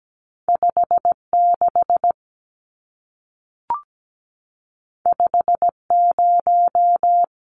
Morse code, heard as a keyed tone, about 17 wpm, 700 Hz. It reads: 56 50
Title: Numbers - 2-Digit 17wpm